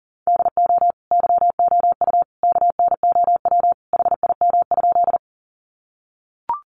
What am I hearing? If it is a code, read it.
Morse code, heard as a keyed tone, about 40 wpm, 700 Hz. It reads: DO YOU KNOW HIM?
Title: Sentences from Top 100 Words 40wpm